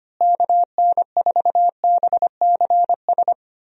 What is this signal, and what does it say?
Morse code, heard as a keyed tone, about 25 wpm, 700 Hz. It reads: KN4BCS
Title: USA Call Signs - 6 Characters - Repeated 2x - 25wpm